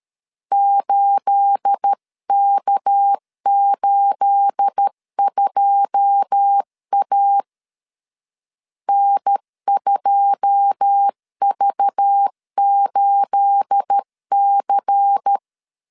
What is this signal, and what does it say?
Morse code, heard as a keyed tone, about 13 wpm, 800 Hz. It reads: 8K82A N2V8C
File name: LEZIONI / 9 28 12wpm 2351